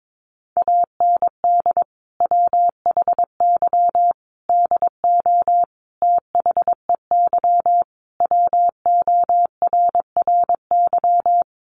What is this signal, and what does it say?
Morse code, heard as a keyed tone, about 22 wpm, 700 Hz. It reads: AND WHY DO THEY WORRY
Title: Sentences from Top 1100 Words 22wpm